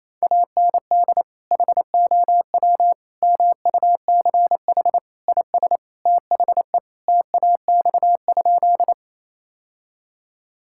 Morse code, heard as a keyed tone, about 28 wpm, 700 Hz. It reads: AND HOW MUCH IS THE TAX?
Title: Sentences from Top 1000 Words - Review Part 5 - 28wpm